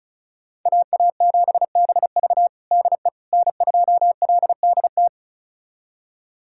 Morse code, heard as a keyed tone, about 36 wpm, 700 Hz. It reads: AA7BV DE N2LDT
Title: USA Callsign DE Callsign - 5 Characters - 35wpm